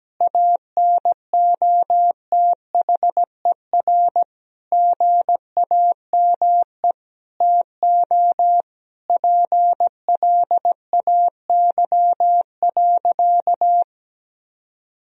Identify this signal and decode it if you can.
Morse code, heard as a keyed tone, about 17 wpm, 700 Hz. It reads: ANOTHER GAME TO PLAY.
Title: Sentences from Top 1000 Words - Review Part 3 - 17wpm